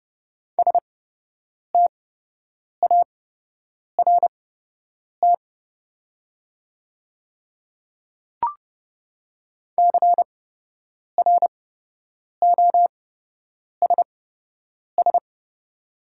Morse code, heard as a keyed tone, about 32 wpm, spaced as for 8 wpm, 700 Hz. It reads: START CROSS
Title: 8x Character Spacing - Top 500 Words - 30wpm